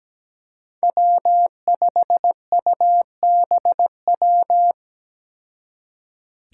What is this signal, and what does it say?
Morse code, heard as a keyed tone, about 17 wpm, 700 Hz. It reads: W5UBW
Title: Five Challenging Letter-Number Combinations 17wpm